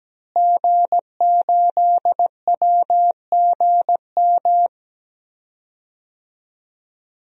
Morse code, heard as a keyed tone, about 17 wpm, 700 Hz. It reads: G8WGM
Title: DXCC-100 Call Signs - 5 Characters - Repeated 3x - 17wpm